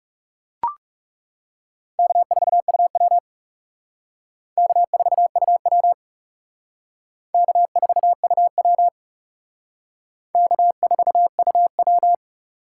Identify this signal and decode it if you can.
Morse code, heard as a keyed tone, about 38 wpm, 700 Hz. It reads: K4UW K4UW K4UW K4UW